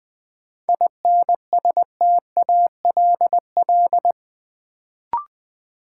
Morse code, heard as a keyed tone, about 20 wpm, 700 Hz. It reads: INSTALL
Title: Sentences from Top 1600 Words - New Word Drill - 20wpm